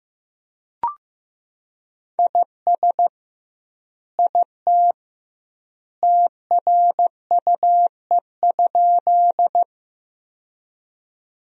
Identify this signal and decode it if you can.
Morse code, heard as a keyed tone, about 16 wpm, 700 Hz. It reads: IS IT TRUE?
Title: 2x Word-Spacing - Sentences from Top 400 Words - 15wpm